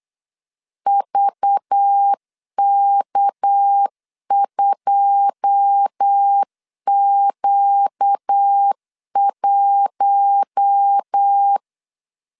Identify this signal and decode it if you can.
Morse code, heard as a keyed tone, about 8 wpm, 800 Hz. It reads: VK2Q1
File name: LEZIONI / 10 SIGNS / 10D COMA 8wpm 3749